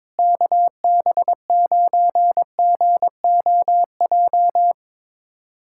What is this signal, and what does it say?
Morse code, heard as a keyed tone, about 22 wpm, 700 Hz. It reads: KB9GOJ
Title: USA Callsign DE Callsign - 6 Characters - 22wpm